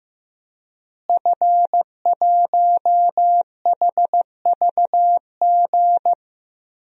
Morse code, heard as a keyed tone, about 16 wpm, 700 Hz. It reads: F1HVG
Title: DXCC-100 Call Signs - 5 Characters - Repeated 2x - 15wpm